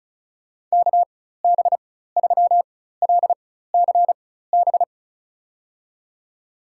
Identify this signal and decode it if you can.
Morse code, heard as a keyed tone, about 36 wpm, spaced as for 16 wpm, 700 Hz. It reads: KB3LCB
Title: USA Call Signs - 6 Characters - 4x Character Spacing - 35wpm